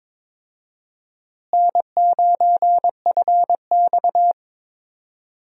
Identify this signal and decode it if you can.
Morse code, heard as a keyed tone, about 22 wpm, 700 Hz. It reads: N9FX
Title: USA Call Signs - 4 Characters - Repeated 2x - 22wpm